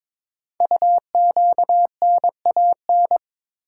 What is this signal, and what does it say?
Morse code, heard as a keyed tone, about 23 wpm, 700 Hz. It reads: UQNAN
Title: Code Groups - 5 Letters - 22wpm